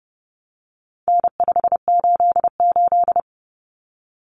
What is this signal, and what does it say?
Morse code, heard as a keyed tone, about 30 wpm, 700 Hz. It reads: N588
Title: Common Cut Numbers - 4-Digit 30wpm